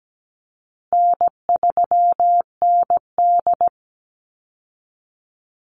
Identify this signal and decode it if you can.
Morse code, heard as a keyed tone, about 17 wpm, 700 Hz. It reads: N3ND
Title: Easy Call Signs Repeated 3x - 17wpm